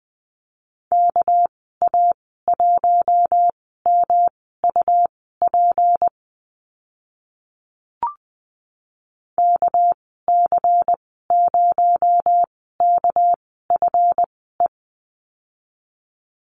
Morse code, heard as a keyed tone, about 20 wpm, spaced as for 14 wpm, 700 Hz. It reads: KA1MUP KC0KFE
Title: 231 Lesson 37g - Letter x - Call Signs 20wpm